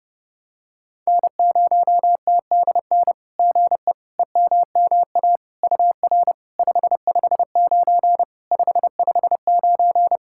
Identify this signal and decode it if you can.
Morse code, heard as a keyed tone, about 29 wpm, 690 Hz. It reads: N0TDN GE EMMA UR 559 559
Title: SOTA QSO Practice 30wpm